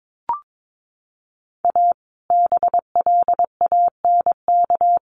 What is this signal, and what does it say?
Morse code, heard as a keyed tone, about 22 wpm, 700 Hz. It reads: A BLANK
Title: Sentences from Top 1500 Words 22wpm